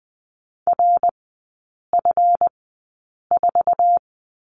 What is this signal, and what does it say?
Morse code, heard as a keyed tone, about 20 wpm, spaced as for 8 wpm, 690 Hz. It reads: RF4
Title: 145 Lesson 24e - Letter y - Three Characters 20wpm